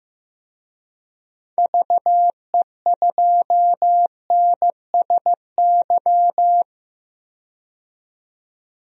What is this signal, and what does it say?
Morse code, heard as a keyed tone, about 15 wpm, 700 Hz. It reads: VE2NSY
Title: DXCC-100 Call Signs - 6 Characters - 15wpm